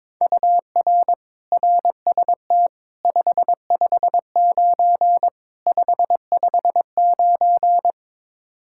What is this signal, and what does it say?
Morse code, heard as a keyed tone, about 22 wpm, 700 Hz. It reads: UR RST 559 559
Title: Practice QSO - Exchange 1 - Beginning - 22wpm